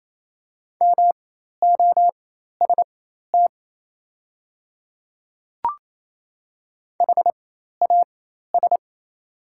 Morse code, heard as a keyed tone, about 28 wpm, spaced as for 13 wpm, 690 Hz. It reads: MOST HAS
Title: 4x Character Spacing - Top 100 Words - 28wpm